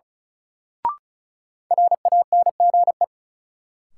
Morse code, heard as a keyed tone, about 34 wpm, 700 Hz. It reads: RANGE
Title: Sentences from Top 900 Words - New Word Drill - 35wpm